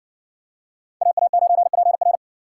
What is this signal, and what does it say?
Morse code, heard as a keyed tone, about 60 wpm, 700 Hz. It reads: AA9PR